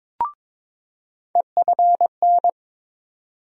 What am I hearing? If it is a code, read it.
Morse code, heard as a keyed tone, about 22 wpm, 700 Hz. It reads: EFN
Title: USA Call Sign Postfixes - 3 Letters 22wpm